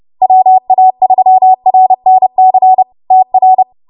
Morse code, heard as a keyed tone, about 30 wpm, 750 Hz. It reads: WA3RNC TR